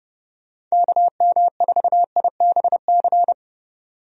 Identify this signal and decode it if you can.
Morse code, heard as a keyed tone, about 30 wpm, 700 Hz. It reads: KM4IBC
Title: USA Call Signs - 6 Characters - 30wpm